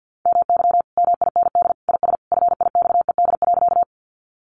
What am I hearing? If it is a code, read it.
Morse code, heard as a keyed tone, about 50 wpm, 700 Hz. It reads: MY MIND IS FIXED.
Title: Sentences from Top 1100 Words 50wpm